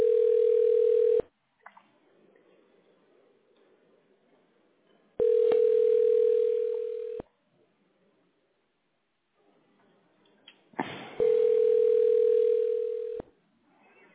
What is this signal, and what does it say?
Ringback tone (North America), 440 + 480 Hz.